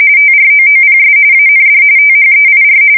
RTTY, about 45.45 baud. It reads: TROOPS AND THEIR
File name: fsk170 / usnavy2